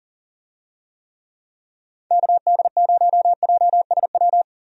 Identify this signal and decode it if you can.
Morse code, heard as a keyed tone, about 40 wpm, 700 Hz. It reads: KD0JSW
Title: USA Call Signs - 6 Characters - Repeated 3x - 40wpm